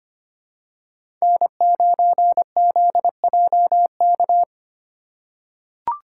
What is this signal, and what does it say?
Morse code, heard as a keyed tone, about 25 wpm, 700 Hz. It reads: N9ZJK
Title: USA Call Signs - 5 Characters - 2x Character Spacing - 25wpm